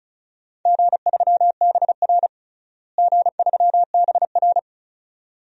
Morse code, heard as a keyed tone, about 36 wpm, 700 Hz. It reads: G3BR G3BR
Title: DXCC-100 Call Signs - 4 Characters - Repeated 2x - 35wpm